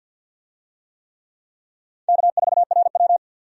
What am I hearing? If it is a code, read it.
Morse code, heard as a keyed tone, about 51 wpm, 700 Hz. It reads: K4RW
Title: USA Call Signs - 4 Characters - 50wpm